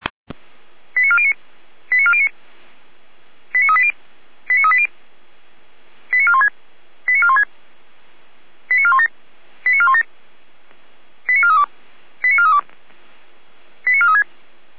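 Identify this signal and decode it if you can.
Five-tone selective call (ZVEI), heard as quick runs of five tones, tones of about 70 ms. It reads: 89409 89409 89380 89380 89416 89416 89417 89417 89432 89432 89436
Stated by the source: ZVEI five-tone selective call received by SDR, from feuerwehr-orlamuende (fire brigade Orlamünde) archive.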